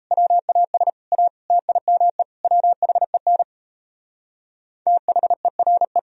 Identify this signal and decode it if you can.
Morse code, heard as a keyed tone, about 36 wpm, 700 Hz. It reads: WAS A TIME WHEN THERE